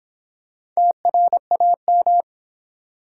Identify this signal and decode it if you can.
Morse code, heard as a keyed tone, about 26 wpm, 700 Hz. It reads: TRAM